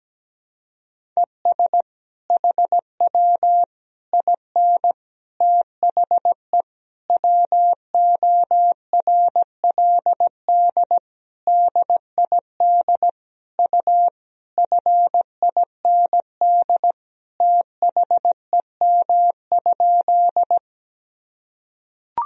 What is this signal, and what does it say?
Morse code, heard as a keyed tone, about 17 wpm, 700 Hz. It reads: ES HW IN THE WORLD DID U FIND THEM?
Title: Sentences from Top 200 Words and Abbreviations 17wpm